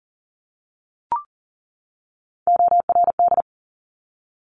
Morse code, heard as a keyed tone, about 42 wpm, 700 Hz. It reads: ORD
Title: Top Three Letter Combinations in Words 40wpm